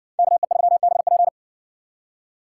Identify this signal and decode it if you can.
Morse code, heard as a keyed tone, about 59 wpm, 700 Hz. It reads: KE3BP